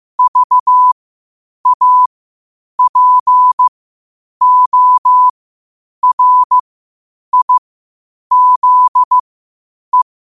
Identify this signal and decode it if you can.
Morse code, heard as a keyed tone, about 15 wpm, spaced as for 8 wpm, 1000 Hz. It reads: VAPORIZE
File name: qotdfiles / 8wpm